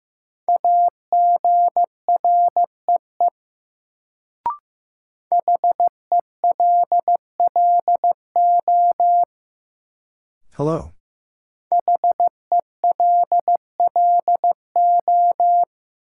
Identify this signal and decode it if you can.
Morse code, heard as a keyed tone, about 15 wpm, 700 Hz. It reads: AGREE HELLO HELLO
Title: Sentences from Top 1100 Words - New Word Drill Alt - 15wpm